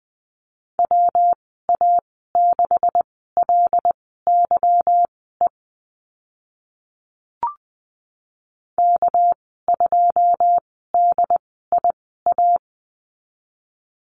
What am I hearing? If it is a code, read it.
Morse code, heard as a keyed tone, about 20 wpm, spaced as for 15 wpm, 700 Hz. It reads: WA6LYE K2DIA
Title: 213 Lesson 34g - Number 8 - Call Signs 20wpm